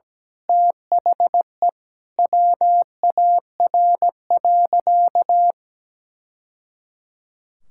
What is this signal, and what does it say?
Morse code, heard as a keyed tone, about 17 wpm, 700 Hz. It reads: THE WAR.
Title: Sentences from Top 1000 Words - Review Part 6 - 17wpm